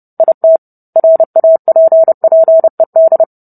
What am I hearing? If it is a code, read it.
Morse code, heard as a keyed tone, about 30 wpm, 650 Hz. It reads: IT RAPPED